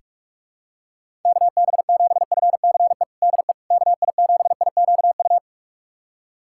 Morse code, heard as a keyed tone, about 47 wpm, 700 Hz. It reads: KB7FCE DE KI7IQU
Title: USA Callsign DE Callsign - 6 Characters - 45wpm